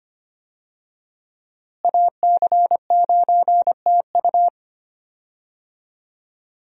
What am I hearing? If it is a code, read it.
Morse code, heard as a keyed tone, about 25 wpm, 700 Hz. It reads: AC9TU
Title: USA Call Signs - 5 Characters - Repeated 2x - 25wpm